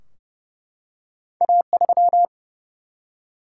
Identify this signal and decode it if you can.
Morse code, heard as a keyed tone, about 30 wpm, 700 Hz. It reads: A3
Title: Single-Letter Single-Number 30wpm